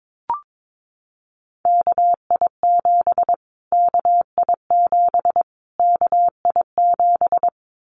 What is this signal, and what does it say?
Morse code, heard as a keyed tone, about 22 wpm, 700 Hz. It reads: KI7 KI7 KI7
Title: Callsign Prefixes for Top 25 DXCC - Repeated 3x - 22wpm